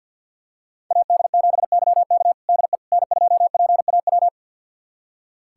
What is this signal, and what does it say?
Morse code, heard as a keyed tone, about 47 wpm, 700 Hz. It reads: AD7YK DE N2PAW
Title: USA Callsign DE Callsign - 5 Characters - 50wpm